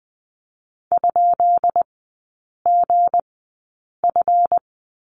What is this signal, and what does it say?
Morse code, heard as a keyed tone, about 19 wpm, spaced as for 8 wpm, 700 Hz. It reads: ?GF
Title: 218 Lesson 35e - Number 0 - Three Characters 20wpm